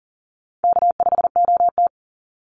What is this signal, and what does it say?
Morse code, heard as a keyed tone, about 40 wpm, 700 Hz. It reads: K5OT
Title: NA SOTA Chaser Callsigns 40wpm